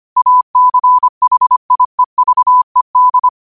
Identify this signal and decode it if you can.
Morse code, heard as a keyed tone, about 25 wpm, 1000 Hz. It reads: ACHIEVED
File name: qotdfiles / 25wpm